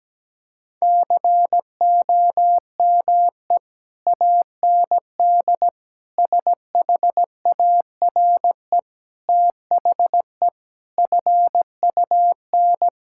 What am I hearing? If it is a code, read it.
Morse code, heard as a keyed tone, about 17 wpm, 700 Hz. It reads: COME AND SHARE THE FUN
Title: Sentences from Top 1000 Words - Review Part 6 - 17wpm